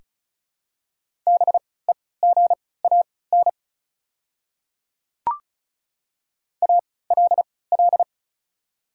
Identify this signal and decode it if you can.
Morse code, heard as a keyed tone, about 36 wpm, spaced as for 20 wpm, 690 Hz. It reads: BEGAN ALL
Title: 3x Character Spacing - Top 300 Words - 35wpm